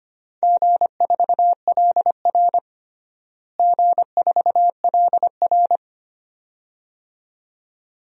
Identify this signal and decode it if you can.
Morse code, heard as a keyed tone, about 25 wpm, 700 Hz. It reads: G4LR G4LR
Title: DXCC-100 Call Signs - 4 Characters - Repeated 2x - 25wpm